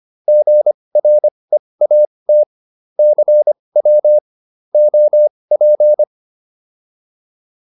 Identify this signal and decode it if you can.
Morse code, heard as a keyed tone, about 25 wpm, spaced as for 20 wpm, 600 Hz. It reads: GREAT CW OP